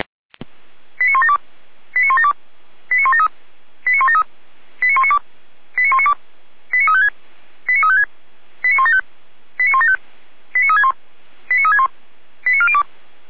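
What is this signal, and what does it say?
Five-tone selective call (ZVEI), heard as quick runs of five tones, tones of about 70 ms. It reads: 89172 89172 89173 89173 89192 89192 89356 89356 89175 89175 89361 89361 89402
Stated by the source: ZVEI five-tone selective call received on an SDR, from the feuerwehr-orlamuende (Orlamünde fire brigade) audio server.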